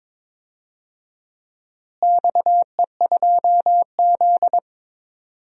Morse code, heard as a keyed tone, about 22 wpm, 700 Hz. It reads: XE2Z
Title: DXCC-100 Call Signs - 4 Characters - Repeated 2x - 22wpm